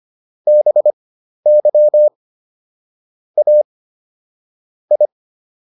Morse code, heard as a keyed tone, about 25 wpm, spaced as for 12 wpm, 600 Hz. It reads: BY A I